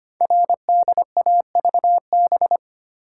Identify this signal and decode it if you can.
Morse code, heard as a keyed tone, about 25 wpm, 700 Hz. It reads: RDAVB
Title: Code Groups - 5 Letters - 25wpm